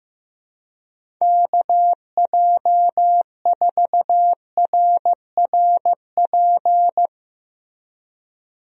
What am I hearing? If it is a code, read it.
Morse code, heard as a keyed tone, about 15 wpm, 700 Hz. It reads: KJ4RRP
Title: USA Call Signs - 6 Characters - Repeated 2x - 15wpm